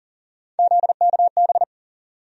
Morse code, heard as a keyed tone, about 40 wpm, 700 Hz. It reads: ZKB
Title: USA Call Sign Postfixes - 3 Letters 40wpm